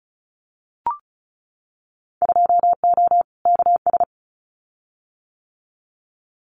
Morse code, heard as a keyed tone, about 34 wpm, 700 Hz. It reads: 2O KS
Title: Winter Field Day Exchanges Encore 35wpm